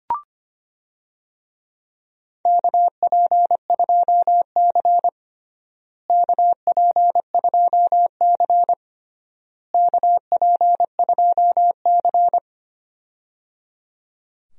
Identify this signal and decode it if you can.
Morse code, heard as a keyed tone, about 25 wpm, 700 Hz. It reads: KP2C KP2C KP2C
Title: DXCC-100 Call Signs - 4 Characters - Repeated 3x - 25wpm